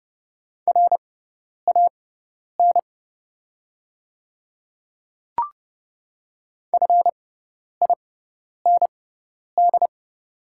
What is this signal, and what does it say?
Morse code, heard as a keyed tone, about 30 wpm, spaced as for 10 wpm, 700 Hz. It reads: RAN FIND
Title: 6x Character Spacing - Top 500 Words - 30wpm